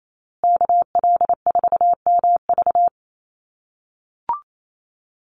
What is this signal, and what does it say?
Morse code, heard as a keyed tone, about 28 wpm, 700 Hz. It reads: KL4MV
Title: USA Call Signs Repeated 2x 28wpm